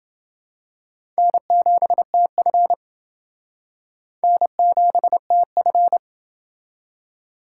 Morse code, heard as a keyed tone, about 29 wpm, 700 Hz. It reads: N7TF N7TF